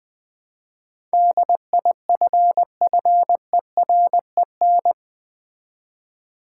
Morse code, heard as a keyed tone, about 20 wpm, 700 Hz. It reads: DIFFEREN
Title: Build A Word - 4 Letters At A Time - 11-Letter Words - 20wpm